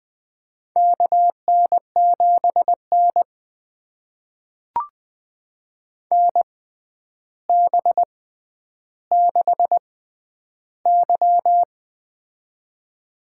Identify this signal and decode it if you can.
Morse code, heard as a keyed tone, about 20 wpm, spaced as for 10 wpm, 700 Hz. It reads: KN7N NB6Y
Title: USA Call Signs - 6x Character Spacing - 20wpm